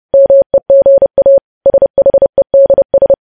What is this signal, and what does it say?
Morse code, heard as a keyed tone, about 30 wpm, 550 Hz. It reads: MEGA SHEDS
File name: podcast / News headlines-1x-30x30